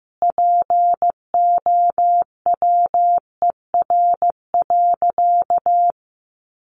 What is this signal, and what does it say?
Morse code, heard as a keyed tone, about 15 wpm, 700 Hz. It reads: POWER.
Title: Sentences from Top 500 Words 15wpm